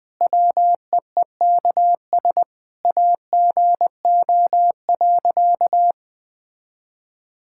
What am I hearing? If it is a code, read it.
Morse code, heard as a keyed tone, about 20 wpm, 700 Hz. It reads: WEEKS AGO.
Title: Sentences with 5-Word Phrases - 20wpm